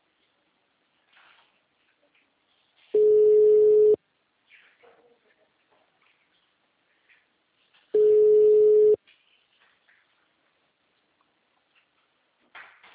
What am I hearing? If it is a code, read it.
Ringback tone (Europe, 425 Hz).
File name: out- -2014